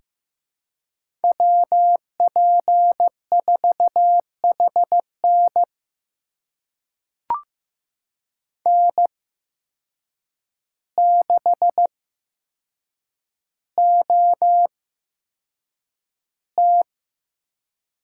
Morse code, heard as a keyed tone, about 15 wpm, spaced as for 7 wpm, 700 Hz. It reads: WP4HN N6OT
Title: USA Call Signs - 5 Characters - 8x Character Spacing - 15wpm